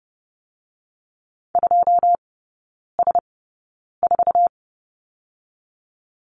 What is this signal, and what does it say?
Morse code, heard as a keyed tone, about 30 wpm, spaced as for 9 wpm, 700 Hz. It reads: 2S4
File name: 081 Lesson 14e - Number 2 - Three Characters 30wpm